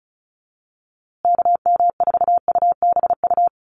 Morse code, heard as a keyed tone, about 34 wpm, 700 Hz. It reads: KM4UBU
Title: USA Call Signs Repeated 3x 35wpm